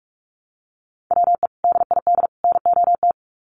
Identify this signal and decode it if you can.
Morse code, heard as a keyed tone, about 44 wpm, 700 Hz. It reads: WE DID NOT
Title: Sets of 3 Words 45wpm